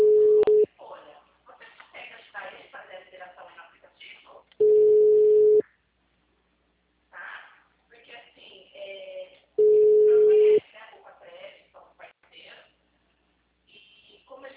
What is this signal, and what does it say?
Ringback tone (Europe, 425 Hz), with speech.
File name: out- -2001- - - .100